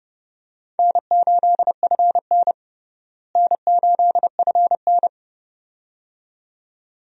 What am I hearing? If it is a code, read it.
Morse code, heard as a keyed tone, about 30 wpm, 700 Hz. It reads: N8FN N8FN
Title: USA Call Signs - 4 Characters - Repeated 2x - 30wpm